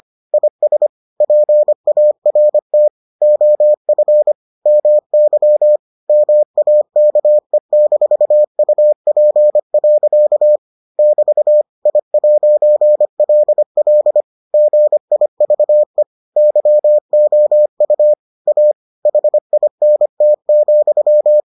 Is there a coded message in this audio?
Yes: Morse code from 0.3 to 21.4 s, heard as a keyed tone, about 25 wpm, 600 Hz. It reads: IS PART OF MY MAKE-UP. <BT> I'LL GIVE YOU A HINT,